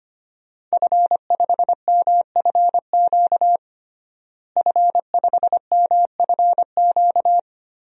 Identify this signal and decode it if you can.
Morse code, heard as a keyed tone, about 25 wpm, 700 Hz. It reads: F5MFQ F5MFQ
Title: DXCC-100 Call Signs - 5 Characters - Repeated 2x - 25wpm